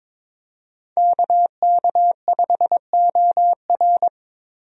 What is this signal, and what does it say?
Morse code, heard as a keyed tone, about 22 wpm, 700 Hz. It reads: KK5OR
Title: USA Call Signs - 5 Characters - Repeated 3x - 22wpm